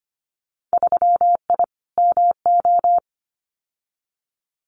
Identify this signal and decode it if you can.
Morse code, heard as a keyed tone, about 25 wpm, 700 Hz. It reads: 3I MO
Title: Winter Field Day Exchanges 25wpm